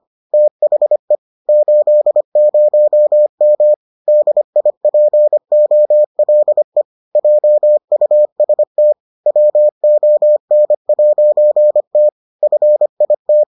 Morse code, heard as a keyed tone, about 25 wpm, 600 Hz. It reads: THE 80M DIPOLE JUST WON'T FIT